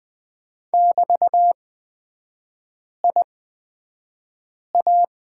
Morse code, heard as a keyed tone, about 20 wpm, spaced as for 5 wpm, 700 Hz. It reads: <BT>IA
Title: 224 Lesson 36e - BT - Three Characters 20wpm